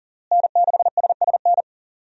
Morse code, heard as a keyed tone, about 37 wpm, 700 Hz. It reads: N6SSN
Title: USA Call Signs - 5 Characters - Repeated 3x - 40wpm